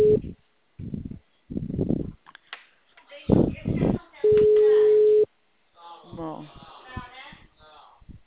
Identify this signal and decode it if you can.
Ringback tone (Europe, 425 Hz), with speech.
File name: out- -2017- - - .79356